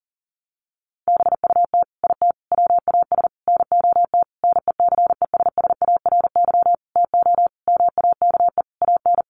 Morse code, heard as a keyed tone, about 40 wpm, 700 Hz. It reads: BUT IT WAS NOT NECESSARY TO MAKE AN
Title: Sentences from Top 1200 Words 40wpm